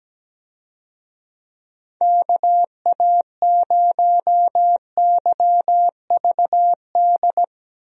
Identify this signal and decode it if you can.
Morse code, heard as a keyed tone, about 17 wpm, 700 Hz. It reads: KA0YVD
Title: USA Call Signs - 6 Characters - 17wpm